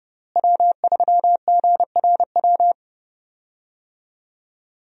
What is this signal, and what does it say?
Morse code, heard as a keyed tone, about 30 wpm, 700 Hz. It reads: W3GRW
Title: USA Call Signs - 5 Characters - Repeated 2x - 30wpm